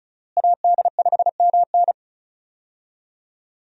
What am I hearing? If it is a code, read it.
Morse code, heard as a keyed tone, about 36 wpm, 700 Hz. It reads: AD5MN
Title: USA Call Signs - 5 Characters - Repeated 3x - 35wpm